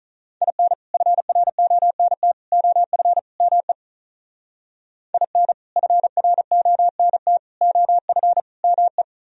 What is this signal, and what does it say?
Morse code, heard as a keyed tone, about 43 wpm, 700 Hz. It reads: IN FRONT OF ME IN FRONT OF ME
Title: Speed-Racing - Sets of 4 Words 35wpm